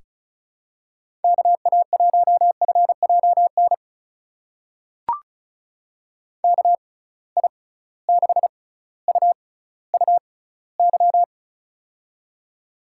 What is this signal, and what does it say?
Morse code, heard as a keyed tone, about 36 wpm, spaced as for 18 wpm, 700 Hz. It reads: KA1FJN KI6UUY
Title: USA Call Signs - 6 Characters - 6x Character Spacing - 35wpm